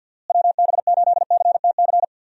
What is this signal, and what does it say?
Morse code, heard as a keyed tone, about 49 wpm, 700 Hz. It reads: WB8CTC
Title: USA Call Signs - 6 Characters - Repeated 3x - 50wpm